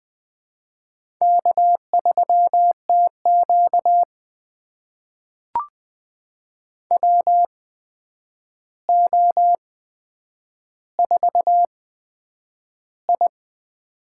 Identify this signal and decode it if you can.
Morse code, heard as a keyed tone, about 20 wpm, spaced as for 9 wpm, 700 Hz. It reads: K3TQ WO4I
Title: USA Call Signs - 4 Characters - 8x Character Spacing - 20wpm